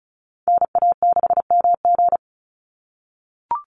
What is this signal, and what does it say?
Morse code, heard as a keyed tone, about 36 wpm, 700 Hz. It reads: NA6MG
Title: NA SOTA Activator Callsigns 35wpm